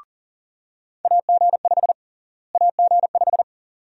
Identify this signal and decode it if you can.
Morse code, heard as a keyed tone, about 40 wpm, 700 Hz. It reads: AG5 AG5
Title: USA Call Sign Prefixes - Repeated 2x - 40wpm